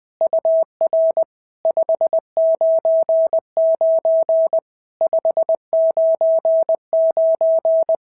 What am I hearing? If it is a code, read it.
Morse code, heard as a keyed tone, about 19 wpm, 650 Hz. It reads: UR 599 599
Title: SOTA QSO Practice 20wpm